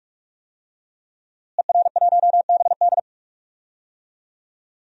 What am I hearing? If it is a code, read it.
Morse code, heard as a keyed tone, about 46 wpm, 700 Hz. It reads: ER1BD
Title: DXCC-100 Call Signs - 5 Characters - Repeated 3x - 45wpm